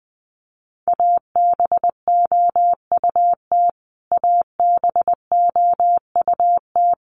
Morse code, heard as a keyed tone, about 20 wpm, 700 Hz. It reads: ABOUT ABOUT
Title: Top 200 Words Repeated 20wpm